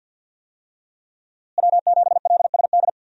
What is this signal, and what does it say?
Morse code, heard as a keyed tone, about 53 wpm, 700 Hz. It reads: W7LSD
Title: USA Call Signs - 5 Characters - Repeated 3x - 50wpm